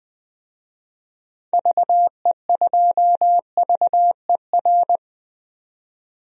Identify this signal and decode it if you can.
Morse code, heard as a keyed tone, about 21 wpm, 700 Hz. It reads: VE2VER
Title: DXCC-100 Call Signs - 6 Characters - 20wpm